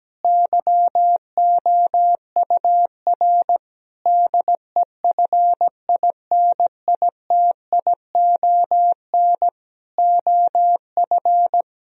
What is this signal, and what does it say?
Morse code, heard as a keyed tone, about 17 wpm, 700 Hz. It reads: YOUR DEFINITION OF
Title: Sentences from Top 1000 Words - Review Part 4 - 17wpm